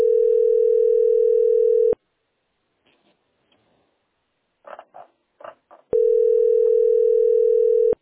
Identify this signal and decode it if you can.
Ringback tone (North America), 440 + 480 Hz.